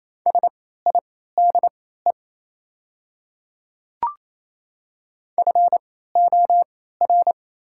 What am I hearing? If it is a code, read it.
Morse code, heard as a keyed tone, about 28 wpm, spaced as for 16 wpm, 700 Hz. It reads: SIDE FOR
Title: 3x Character Spacing - Top 200 Words - 28wpm